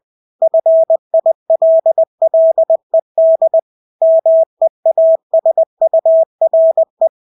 Morse code, heard as a keyed tone, about 20 wpm, 650 Hz. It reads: FILLED MEASURE
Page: Engelske ord 20 wpm